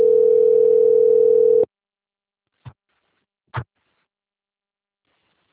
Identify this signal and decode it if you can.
Ringback tone (North America), 440 + 480 Hz.